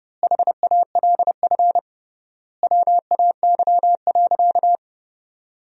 Morse code, heard as a keyed tone, about 30 wpm, 700 Hz. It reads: HALF WAY.
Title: 3x Word-Spacing - Sentences from Top 400 Words - 30wpm